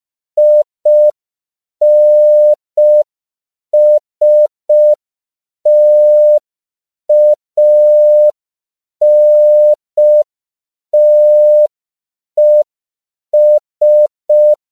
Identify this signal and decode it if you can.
Morse code, heard as a keyed tone, about 5 wpm, 600 Hz. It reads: INSTANTES
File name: CW / 5PPM